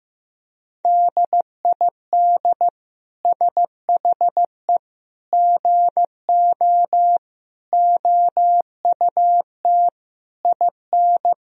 Morse code, heard as a keyed tone, about 15 wpm, 700 Hz. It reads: DID SHE GO OUT IN
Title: Sentences from Top 1000 Words - Review Part 3 - 15wpm